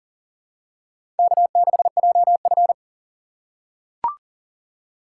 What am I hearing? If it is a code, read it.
Morse code, heard as a keyed tone, about 40 wpm, 700 Hz. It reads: K6JF
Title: USA Call Signs - 4 Characters - 3x Character Spacing - 40wpm